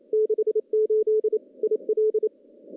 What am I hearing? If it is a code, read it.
Morse code, heard as a keyed tone, about 28 wpm, 430 Hz. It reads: 68 IL
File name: contests / CW-SS-k5tr- -0400